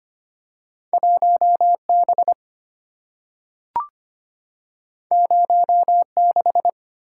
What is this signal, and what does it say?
Morse code, heard as a keyed tone, about 25 wpm, 700 Hz. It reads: 1 B 0 6
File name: Two Letter-Number Combinations - 25wpm